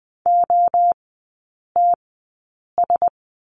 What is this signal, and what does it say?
Morse code, heard as a keyed tone, about 20 wpm, spaced as for 8 wpm, 700 Hz. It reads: OTS
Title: 035 Lesson 7e - Letter s - Three Characters 20wpm